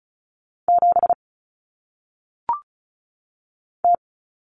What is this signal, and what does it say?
Morse code, heard as a keyed tone, about 34 wpm, 700 Hz. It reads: 7 T
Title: Common Cut Numbers - 1-Digit 35wpm